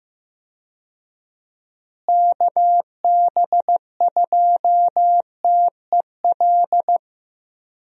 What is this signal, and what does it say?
Morse code, heard as a keyed tone, about 15 wpm, 700 Hz. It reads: KB2TEL
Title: USA Call Signs - 6 Characters - Repeated 3x - 15wpm